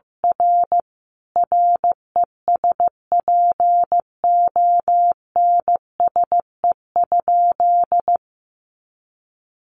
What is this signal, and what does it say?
Morse code, heard as a keyed tone, about 15 wpm, 700 Hz. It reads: R RESPONSE?
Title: Sentences from Top 1400 Words 15wpm